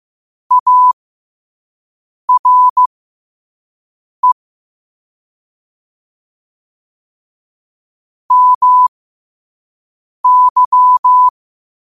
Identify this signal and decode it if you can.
Morse code, heard as a keyed tone, about 15 wpm, spaced as for 5 wpm, 1000 Hz. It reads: ARE MY
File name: qotdfiles / 5wpm